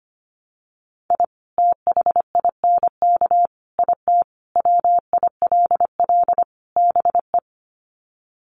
Morse code, heard as a keyed tone, about 25 wpm, 700 Hz. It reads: I THINK IT WILL BE
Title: Sets of 5 Words Encore2 25wpm